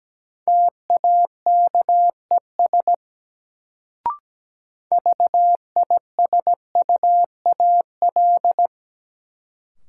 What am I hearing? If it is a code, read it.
Morse code, heard as a keyed tone, about 17 wpm, 700 Hz. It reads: TAKES VISUAL
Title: Sets of 5 Words Encore - New Word Drill - 17wpm